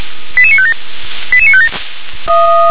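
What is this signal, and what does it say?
Five-tone selective call (ZVEI), heard as quick runs of five tones, tones of about 70 ms. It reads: 88957 88957
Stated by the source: ZVEI five-tone selective call 88786 received by SDR.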